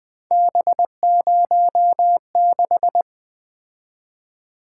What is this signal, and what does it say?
Morse code, heard as a keyed tone, about 23 wpm, 700 Hz. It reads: B06
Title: Speed-Racing - Three Letter-Number Combinations - 20wpm